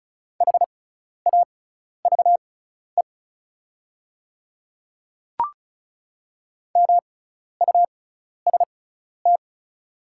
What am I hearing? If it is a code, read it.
Morse code, heard as a keyed tone, about 34 wpm, spaced as for 12 wpm, 690 Hz. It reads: HAVE MUST
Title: 6x Character Spacing - Top 500 Words - 35wpm